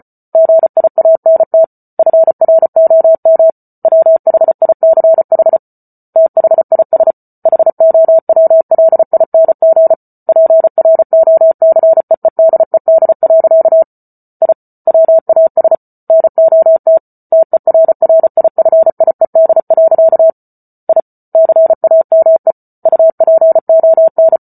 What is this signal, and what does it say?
Morse code, heard as a keyed tone, about 34 wpm, 650 Hz. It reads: GIANT FROM WHICH THIS HOWLING PROCEEDED. I WAS NOT TERRIFIED. I CAME UPON